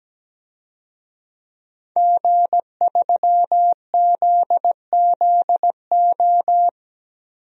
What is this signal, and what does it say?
Morse code, heard as a keyed tone, about 17 wpm, 700 Hz. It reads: G3ZZO
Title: DXCC-100 Call Signs - 5 Characters - 17wpm